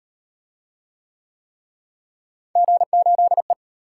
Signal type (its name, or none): Morse code